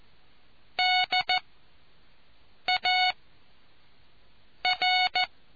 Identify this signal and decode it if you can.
Morse code, heard as a keyed tone, about 14 wpm, 740 Hz. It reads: D A R